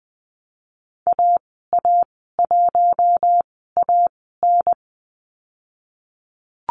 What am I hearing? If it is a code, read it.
Morse code, heard as a keyed tone, about 19 wpm, spaced as for 14 wpm, 700 Hz. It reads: AA1AN
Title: 055 Lesson 10g - Letter r - Call Signs 20wpm